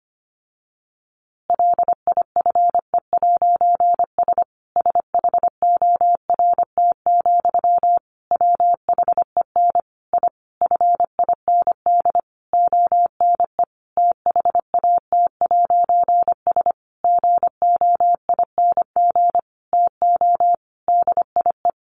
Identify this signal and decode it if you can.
Morse code, heard as a keyed tone, about 25 wpm, 700 Hz. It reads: LIFE'S SHORT, WHEN I FIND ONE THAT'S GOING TO DIE
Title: A Sound Of Thunder 25wpm